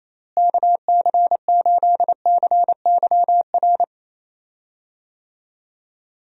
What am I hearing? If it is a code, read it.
Morse code, heard as a keyed tone, about 28 wpm, 700 Hz. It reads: KC8CYR
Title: USA Call Signs - 6 Characters - Repeated 3x - 28wpm